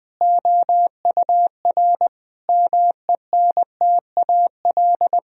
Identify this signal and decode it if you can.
Morse code, heard as a keyed tone, about 20 wpm, 700 Hz. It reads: OUR MENTAL
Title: Sentences from Top 1000 Words - Review Part 4 - 20wpm